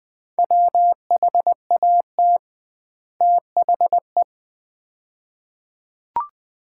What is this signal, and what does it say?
Morse code, heard as a keyed tone, about 21 wpm, 700 Hz. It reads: WHAT THE
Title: 2x Word-Spacing - Sets of 2 Words - 20wpm